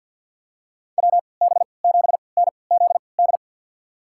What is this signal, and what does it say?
Morse code, heard as a keyed tone, about 50 wpm, spaced as for 28 wpm, 700 Hz. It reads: WB7NZD
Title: USA Call Signs - 3x Character Spacing - 50wpm